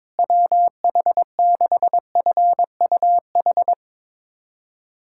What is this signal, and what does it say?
Morse code, heard as a keyed tone, about 22 wpm, 700 Hz. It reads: WH6FUH
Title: USA Callsign DE Callsign 22wpm